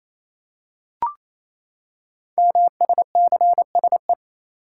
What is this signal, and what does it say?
Morse code, heard as a keyed tone, about 29 wpm, 700 Hz. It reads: MSCSE
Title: Code Groups - 5 Letter:Numbers - 28wpm